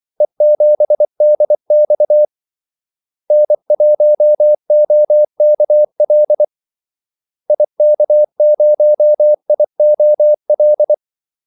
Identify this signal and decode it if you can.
Morse code, heard as a keyed tone, about 24 wpm, 600 Hz. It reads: E7DX N1OKL IK0IOL
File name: Callsigns-24wpm